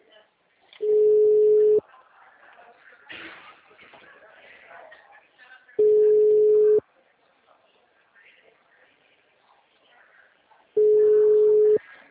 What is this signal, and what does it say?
Ringback tone (Europe, 425 Hz), with speech.
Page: Araras 4